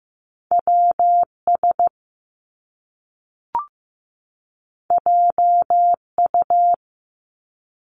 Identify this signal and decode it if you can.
Morse code, heard as a keyed tone, about 15 wpm, 700 Hz. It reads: WS JU
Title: Top Two Letter Combinations in Words 15wpm